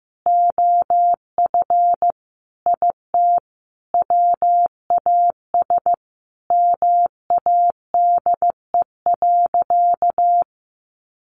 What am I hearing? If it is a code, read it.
Morse code, heard as a keyed tone, about 15 wpm, 700 Hz. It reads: OF IT WAS MADE.
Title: Sentences from Top 1400 Words 15wpm